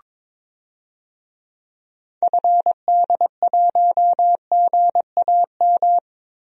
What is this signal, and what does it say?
Morse code, heard as a keyed tone, about 23 wpm, 700 Hz. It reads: FD1GAM
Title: DXCC-100 Call Signs - 6 Characters - Repeated 2x - 22wpm